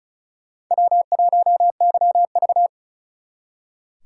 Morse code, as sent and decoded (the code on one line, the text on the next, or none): .-- .---- -.-- ...-
W1YV